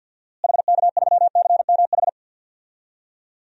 Morse code, heard as a keyed tone, about 52 wpm, 700 Hz. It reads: HK3CMH
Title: DXCC-100 Call Signs - 6 Characters - Repeated 3x - 50wpm